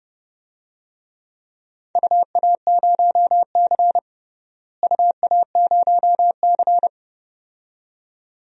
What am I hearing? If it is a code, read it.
Morse code, heard as a keyed tone, about 31 wpm, 700 Hz. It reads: UA0C UA0C
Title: DXCC-100 Call Signs - 4 Characters - Repeated 2x - 30wpm